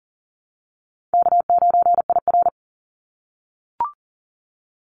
Morse code, heard as a keyed tone, about 40 wpm, 700 Hz. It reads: K9IR
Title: NA SOTA Activator Callsigns 40wpm